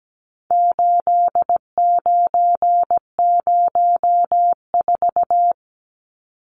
Morse code, heard as a keyed tone, about 17 wpm, 700 Hz. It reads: 8904
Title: Numbers - 4-Digit 17wpm